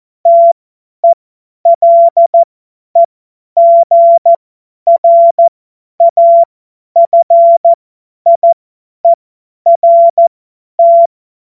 Morse code, heard as a keyed tone, about 14 wpm, spaced as for 10 wpm, 680 Hz. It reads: TELEGRAFIERT
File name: Morsekurs / Witze / Witz 09. August 2022 50 F70